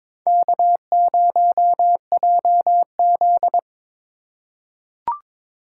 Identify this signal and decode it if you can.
Morse code, heard as a keyed tone, about 22 wpm, 700 Hz. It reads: K0JZ
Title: USA Call Signs - 4x Character Spacing - 22wpm